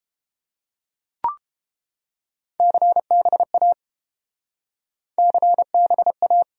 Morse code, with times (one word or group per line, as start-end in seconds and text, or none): 2.6-3.7 CBA
5.2-6.4 CBA